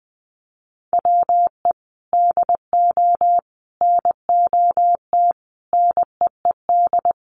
Morse code, heard as a keyed tone, about 19 wpm, 700 Hz. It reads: WE DO NOT NEED
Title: Sets of 5 Words Encore 20wpm